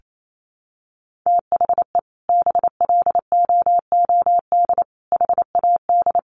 Morse code, heard as a keyed tone, about 28 wpm, 700 Hz. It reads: THE BLOOD HAD
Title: Sentences from Top 1300 Words 28wpm